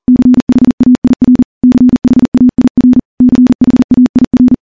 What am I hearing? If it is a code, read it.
Morse code, heard as a keyed tone, about 40 wpm, 250 Hz. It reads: CHAIR CHAIR CHAIR
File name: morse-training-40WPM